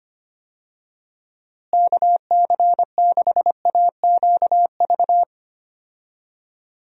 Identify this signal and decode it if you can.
Morse code, heard as a keyed tone, about 25 wpm, 700 Hz. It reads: KC6AQV
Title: USA Call Signs - 6 Characters - Repeated 2x - 25wpm